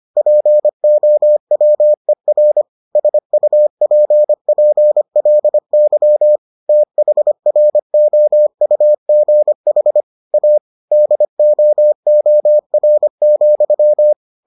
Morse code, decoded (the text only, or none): POWER SUPPLY THROUGH A DOOR,